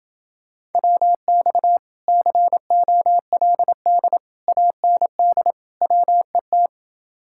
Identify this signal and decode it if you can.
Morse code, heard as a keyed tone, about 28 wpm, 700 Hz. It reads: WX COLD AND WET